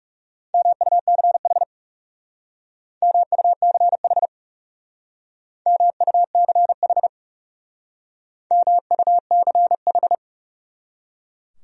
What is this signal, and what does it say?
Morse code, heard as a keyed tone, about 38 wpm, 700 Hz. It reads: MUCH MUCH MUCH MUCH